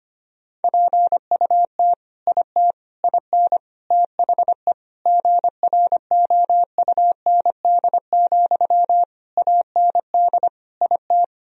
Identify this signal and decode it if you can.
Morse code, heard as a keyed tone, about 25 wpm, 700 Hz. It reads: PUT IT IN THE GROUND, AND IT
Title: Sentences from Top 1000 Words - Review Part 2 - 25wpm